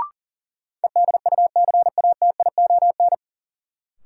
Morse code, heard as a keyed tone, about 40 wpm, 700 Hz. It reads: EDUCATION
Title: Sentences from Top 1100 Words - New Word Drill Alt - 40wpm